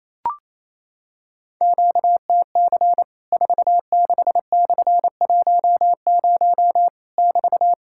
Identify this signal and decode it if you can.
Morse code, heard as a keyed tone, about 28 wpm, 700 Hz. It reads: QTC 46/10 <BT>
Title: Worked All Europe DX Contest - QTC Exchange - 28wpm